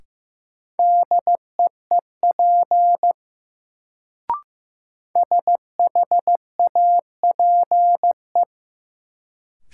Morse code, heard as a keyed tone, about 15 wpm, 700 Hz. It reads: DEEP SHAPE
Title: Sentences from Top 500 Words - New Word Drill - 15wpm